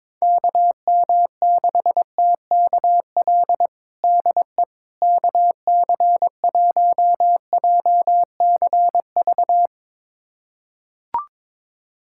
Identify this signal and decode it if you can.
Morse code, heard as a keyed tone, about 22 wpm, 700 Hz. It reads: KM6TKL DE KC1JCV
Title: USA Callsign DE Callsign 22wpm